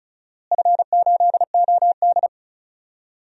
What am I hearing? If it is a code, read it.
Morse code, heard as a keyed tone, about 36 wpm, 700 Hz. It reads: F8OD
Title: DXCC-100 Call Signs - 4 Characters - Repeated 2x - 35wpm